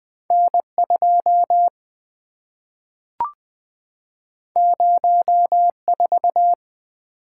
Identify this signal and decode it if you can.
Morse code, heard as a keyed tone, about 20 wpm, 700 Hz. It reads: N2 04
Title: Two Letter-Number Combinations - 20wpm